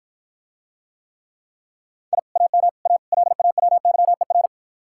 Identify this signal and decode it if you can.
Morse code, heard as a keyed tone, about 54 wpm, 700 Hz. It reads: I AM A LAWYER